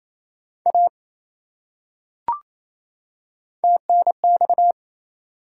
Morse code, heard as a keyed tone, about 27 wpm, 700 Hz. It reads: A TNX
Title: Top 180 QSO Elements 28wpm